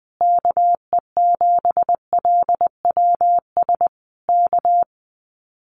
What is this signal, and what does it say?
Morse code, heard as a keyed tone, about 20 wpm, 700 Hz. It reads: KE7LWS K
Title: Common QSO Phrases Encore 20wpm